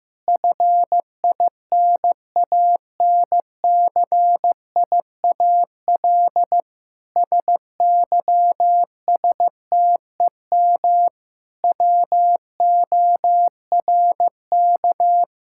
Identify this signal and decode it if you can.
Morse code, heard as a keyed tone, about 15 wpm, 700 Hz. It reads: FINANCIAL SYSTEM WORK
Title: Sentences from Top 1000 Words - Review Part 5 - 15wpm